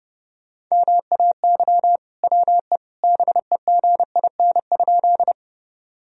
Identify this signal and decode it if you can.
Morse code, heard as a keyed tone, about 30 wpm, 700 Hz. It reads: MAY WE BEGIN?
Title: Sentences from Top 1000 Words - Review Part 1 - 30wpm